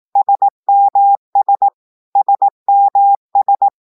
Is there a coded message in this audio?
Yes: Morse code from 0.2 to 3.7 s, heard as a keyed tone, about 18 wpm, 800 Hz. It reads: SMS SMS